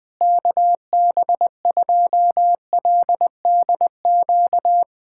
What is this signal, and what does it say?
Morse code, heard as a keyed tone, about 21 wpm, 700 Hz. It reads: KB2LDQ
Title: USA Callsign DE Callsign - 6 Characters - 20wpm